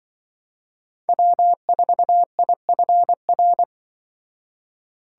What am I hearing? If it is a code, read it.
Morse code, heard as a keyed tone, about 23 wpm, 700 Hz. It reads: W4IFR
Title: Speed Racing - USA 1x3 Call Signs 20wpm